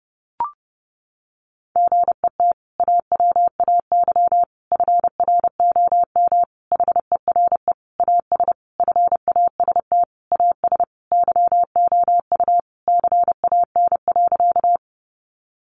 Morse code, heard as a keyed tone, about 30 wpm, 700 Hz. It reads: GET AWAY FROM HERE AS FAST AS YOU CAN.
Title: Sentences from Top 600 Words 30wpm